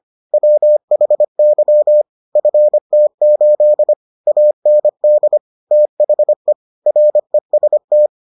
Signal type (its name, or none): Morse code